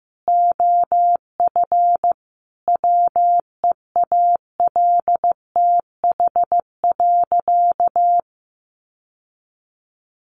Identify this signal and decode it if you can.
Morse code, heard as a keyed tone, about 15 wpm, 700 Hz. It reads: OF WEALTH.
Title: Sentences from Top 1200 Words 15wpm